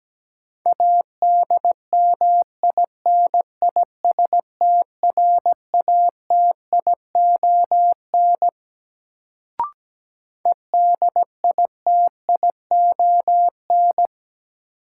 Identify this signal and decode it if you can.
Morse code, heard as a keyed tone, about 17 wpm, 700 Hz. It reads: ADMINISTRATION EDITION
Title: Sets of 5 Words - New Word Drill - 17wpm